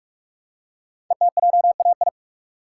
Morse code, heard as a keyed tone, about 44 wpm, 700 Hz. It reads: ETJAI
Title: Code Groups - 5 Letters - 45wpm